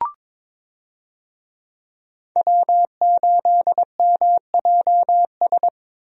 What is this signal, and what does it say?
Morse code, heard as a keyed tone, about 22 wpm, 700 Hz. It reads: W8MJS
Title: USA Call Signs - 5 Characters - Repeated 2x - 22wpm